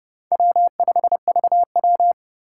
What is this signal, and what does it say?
Morse code, heard as a keyed tone, about 32 wpm, 700 Hz. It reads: W5VW